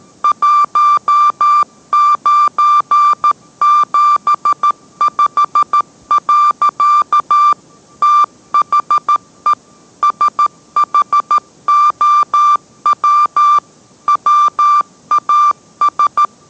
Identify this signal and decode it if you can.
Morse code, heard as a keyed tone, about 13 wpm, 1180 Hz. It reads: 1975. THE SHOW WAS